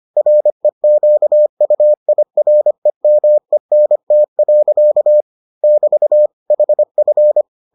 Morse code, heard as a keyed tone, about 25 wpm, 600 Hz. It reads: REQUIREMENT. <BT> HF